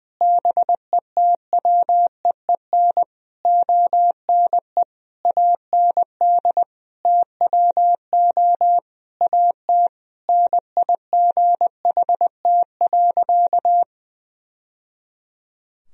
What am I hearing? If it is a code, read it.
Morse code, heard as a keyed tone, about 21 wpm, 700 Hz. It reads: BETWEEN ONE AND TWO AT NIGHT.
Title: Sentences from Top 1000 Words - Review Part 1 - 20wpm